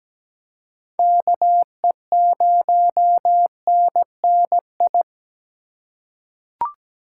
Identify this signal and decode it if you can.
Morse code, heard as a keyed tone, about 17 wpm, 700 Hz. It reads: KE0NNI
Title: USA Call Signs - 4x Character Spacing - 17wpm